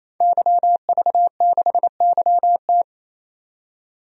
Morse code, heard as a keyed tone, about 28 wpm, 700 Hz. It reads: YV6YT
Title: DXCC-100 Call Signs - 5 Characters - 28wpm